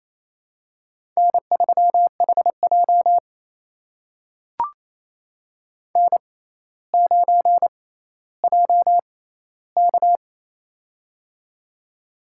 Morse code, heard as a keyed tone, about 29 wpm, spaced as for 15 wpm, 700 Hz. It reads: N3HJ N9JK
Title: USA Call Signs - 4 Characters - 6x Character Spacing - 28wpm